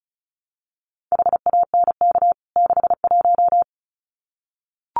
Morse code, heard as a keyed tone, about 34 wpm, 700 Hz. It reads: HANK 61
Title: CWT Exchange 35wpm